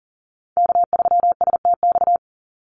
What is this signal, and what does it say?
Morse code, heard as a keyed tone, about 40 wpm, 700 Hz. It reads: K3STX
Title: Call Signs Repeated 3x - 40wpm